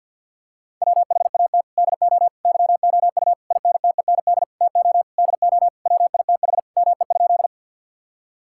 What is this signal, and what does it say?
Morse code, heard as a keyed tone, about 50 wpm, 700 Hz. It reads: WHAT DO YOU INTEND TO DO WITH ME?
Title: Sentences from Top 1600 Words 50wpm